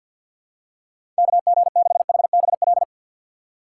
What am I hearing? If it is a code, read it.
Morse code, heard as a keyed tone, about 51 wpm, 700 Hz. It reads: KG6HBL
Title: USA Call Signs - 6 Characters - 6x Character Spacing - 50wpm